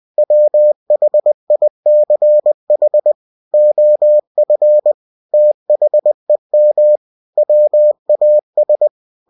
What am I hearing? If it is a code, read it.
Morse code, heard as a keyed tone, about 20 wpm, 600 Hz. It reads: WHICH OF THEM WAS
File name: cw-books / winnie-the-pooh / winnie-20wpm-page-0095